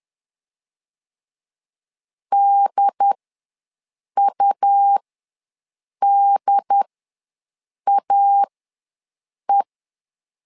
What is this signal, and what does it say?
Morse code, heard as a keyed tone, about 10 wpm, spaced as for 6 wpm, 800 Hz. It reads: DUDAE
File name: LEZIONI / 2 UHDN / 2 UHD ws 10wpm 2058